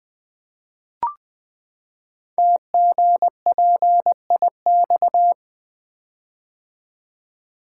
Morse code, heard as a keyed tone, about 20 wpm, 700 Hz. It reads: TGPIX
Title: Code Groups - 5 Letters - 20wpm